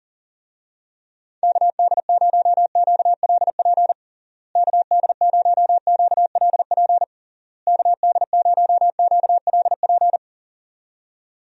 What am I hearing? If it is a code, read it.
Morse code, heard as a keyed tone, about 40 wpm, 700 Hz. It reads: KD0QLP KD0QLP KD0QLP
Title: USA Call Signs - 6 Characters - Repeated 3x - 40wpm